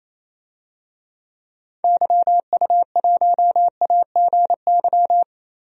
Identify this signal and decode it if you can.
Morse code, heard as a keyed tone, about 28 wpm, 700 Hz. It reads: YU1AGY
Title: DXCC-100 Call Signs - 6 Characters - 28wpm